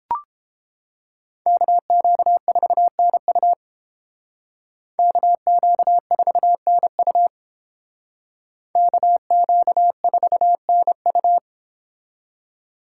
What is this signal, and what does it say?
Morse code, heard as a keyed tone, about 31 wpm, 700 Hz. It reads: KQ4NU KQ4NU KQ4NU